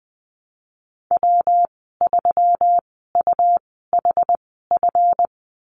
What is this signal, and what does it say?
Morse code, heard as a keyed tone, about 20 wpm, spaced as for 14 wpm, 700 Hz. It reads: W3UHF
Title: 193 Lesson 31g - Letter v - Call Signs 20wpm